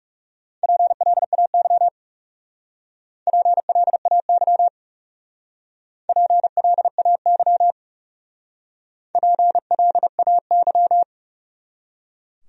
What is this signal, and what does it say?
Morse code, heard as a keyed tone, about 39 wpm, 700 Hz. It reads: PLAY PLAY PLAY PLAY